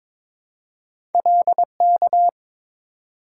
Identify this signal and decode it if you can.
Morse code, heard as a keyed tone, about 22 wpm, 700 Hz. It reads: LK
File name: practice / 22wpm / Two Letter-Number Combinations - 22wpm